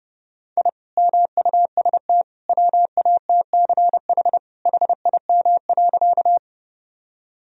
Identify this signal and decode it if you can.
Morse code, heard as a keyed tone, about 30 wpm, 700 Hz. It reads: I MUST WATCH HIM.
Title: Sentences from Top 1000 Words - Review Part 1 - 30wpm